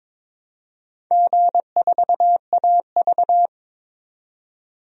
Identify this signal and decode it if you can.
Morse code, heard as a keyed tone, about 22 wpm, 700 Hz. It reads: G4AV
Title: DXCC-100 Call Signs - 4 Characters - 22wpm